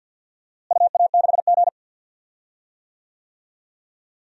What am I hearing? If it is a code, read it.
Morse code, heard as a keyed tone, about 52 wpm, 700 Hz. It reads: UA6G
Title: DXCC-100 Call Signs - 4 Characters - Repeated 3x - 50wpm